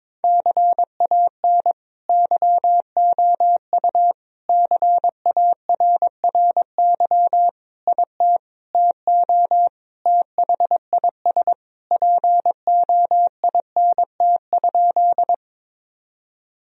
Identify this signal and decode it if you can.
Morse code, heard as a keyed tone, about 22 wpm, 700 Hz. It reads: CAN YOU CARRY IT TO THIS POINT?
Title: Sentences from Top 1000 Words - Review Part 1 - 22wpm